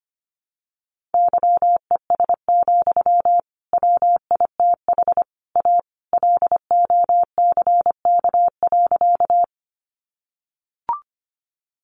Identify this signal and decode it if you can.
Morse code, heard as a keyed tone, about 25 wpm, 700 Hz. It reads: YES, WITH A LOCK.
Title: Sentences from Top 1300 Words 25wpm